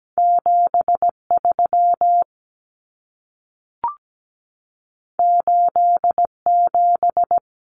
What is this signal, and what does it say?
Morse code, heard as a keyed tone, about 17 wpm, 700 Hz. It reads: 73 87
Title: Common Cut Numbers - 2-Digit 17wpm